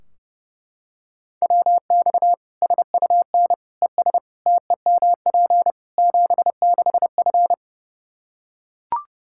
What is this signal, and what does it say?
Morse code, heard as a keyed tone, about 32 wpm, 700 Hz. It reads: WX SUN ES TEMP 76F
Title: K1USN SST QSO Practice 30wpm